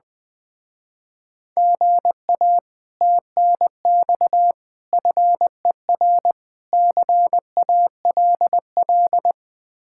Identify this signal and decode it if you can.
Morse code, heard as a keyed tone, about 20 wpm, 700 Hz. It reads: GA TNX FER CALL
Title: Speed-Racing Common QSO Phrases Encore - 20wpm